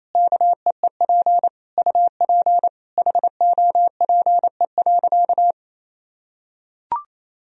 Morse code, heard as a keyed tone, about 28 wpm, 700 Hz. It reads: KEEP UP HOPE.
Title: Sentences from Top 1000 Words - Review Part 1 - 28wpm